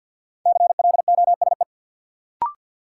Morse code, heard as a keyed tone, about 53 wpm, 700 Hz. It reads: CLOSE
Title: Sentences from Top 300 Words - New Word Drill - 50wpm